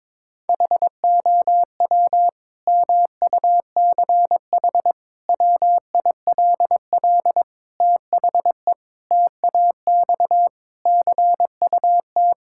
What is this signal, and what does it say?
Morse code, heard as a keyed tone, about 22 wpm, 700 Hz. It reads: HOW MUCH WILL THE TAX CUT
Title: Sentences from Top 1000 Words - Review Part 5 - 22wpm